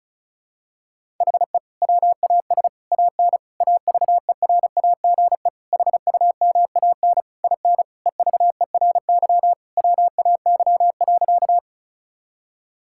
Morse code, heard as a keyed tone, about 36 wpm, 700 Hz. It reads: HE WAS AN AVERAGE HUMAN IN EVERY WAY.
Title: Sentences from Top 1000 Words - Review Part 6 - 35wpm